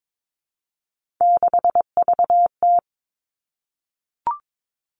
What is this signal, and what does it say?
Morse code, heard as a keyed tone, about 22 wpm, 700 Hz. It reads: 6VT
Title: Three Challenging Letter-Number Combinations 22wpm